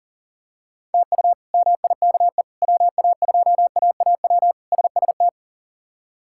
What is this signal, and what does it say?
Morse code, heard as a keyed tone, about 40 wpm, 700 Hz. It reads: TU MIKE WA2AAW SST
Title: K1USN SST QSO Practice - 40wpm